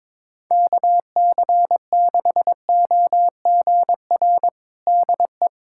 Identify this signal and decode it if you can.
Morse code, heard as a keyed tone, about 22 wpm, 700 Hz. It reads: KC6OGR DE
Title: USA Callsign DE Callsign 22wpm